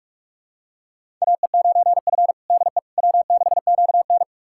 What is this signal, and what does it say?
Morse code, heard as a keyed tone, about 46 wpm, 700 Hz. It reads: AE9F DE W6QN
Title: USA Callsign DE Callsign - 4 Characters - 45wpm